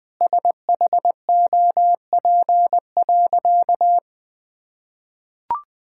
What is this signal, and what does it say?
Morse code, heard as a keyed tone, about 20 wpm, 700 Hz. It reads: SHOP.
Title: Sentences from Top 1600 Words 20wpm